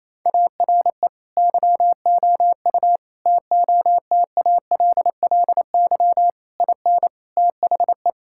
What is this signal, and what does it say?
Morse code, heard as a keyed tone, about 28 wpm, 700 Hz. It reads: ARE YOU TOTALLY IN THE
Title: Sentences from Top 1600 Words 28wpm